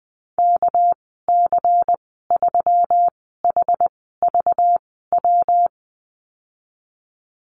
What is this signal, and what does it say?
Morse code, heard as a keyed tone, about 20 wpm, spaced as for 14 wpm, 700 Hz. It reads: KC3HVW
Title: 245 Lesson 39g - Letter z - Call Signs 20wpm